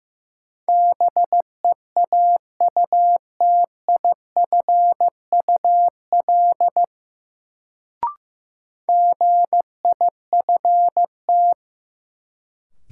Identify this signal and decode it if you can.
Morse code, heard as a keyed tone, about 16 wpm, 700 Hz. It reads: BEAUTIFUL GIFT
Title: Sentences from Top 900 Words - New Word Drill - 15wpm